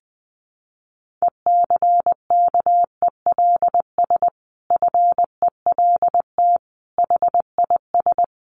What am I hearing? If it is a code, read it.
Morse code, heard as a keyed tone, about 20 wpm, 700 Hz. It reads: ECKELS FELT HIS
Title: A Sound Of Thunder 20wpm